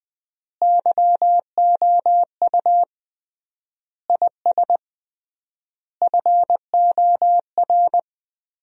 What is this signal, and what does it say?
Morse code, heard as a keyed tone, about 21 wpm, 700 Hz. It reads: YOU IS FOR
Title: 3x Word-Spacing - Sentences from Top 400 Words - 20wpm